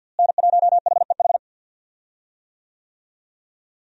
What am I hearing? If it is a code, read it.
Morse code, heard as a keyed tone, about 52 wpm, 700 Hz. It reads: N1HEH
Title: USA Call Signs - 5 Characters - Repeated 2x - 50wpm